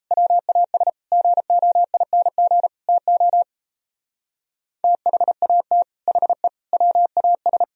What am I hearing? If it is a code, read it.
Morse code, heard as a keyed tone, about 35 wpm, 700 Hz. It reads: WAS GOING TO THAT HE WAS
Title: Speed-Racing - Sets of 5 Words 28wpm